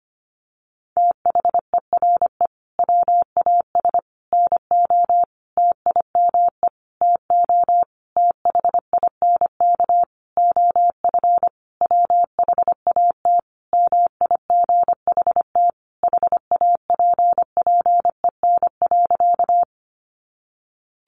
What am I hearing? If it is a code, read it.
Morse code, heard as a keyed tone, about 25 wpm, 700 Hz. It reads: THERE WAS NO TIME TO THINK OF WHAT MIGHT HAPPEN.
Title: Sentences from Top 400 Words 25wpm